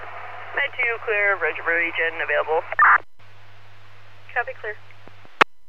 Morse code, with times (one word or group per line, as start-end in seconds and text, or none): none